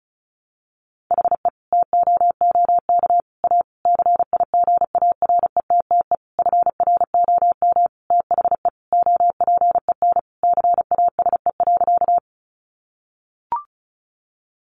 Morse code, heard as a keyed tone, about 34 wpm, 700 Hz. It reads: HE TOOK A CIGARETTE FROM THE OPEN CASE.
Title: Sentences from Top 1000 Words 35wpm